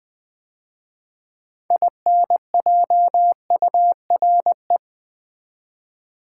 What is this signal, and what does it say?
Morse code, heard as a keyed tone, about 20 wpm, 700 Hz. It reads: INJURE
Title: 6 Letter Words - First 3 Letters Given - 20wpm